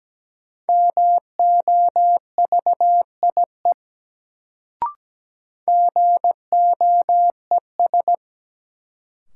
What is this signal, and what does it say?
Morse code, heard as a keyed tone, about 17 wpm, 700 Hz. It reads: MOVIE GOES
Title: Sentences from Top 1000 Words - Review Part 5 - New Word Drill - 17wpm